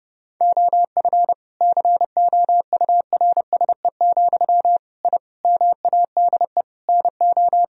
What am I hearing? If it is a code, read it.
Morse code, heard as a keyed tone, about 30 wpm, 700 Hz. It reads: OF COURSE, I MADE NO
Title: Sentences from Top 1000 Words - Review Part 1 - 30wpm